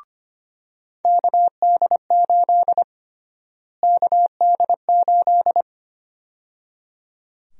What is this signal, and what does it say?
Morse code, heard as a keyed tone, about 25 wpm, 700 Hz. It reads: KD8 KD8
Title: USA Call Sign Prefixes - Repeated 2x - 25wpm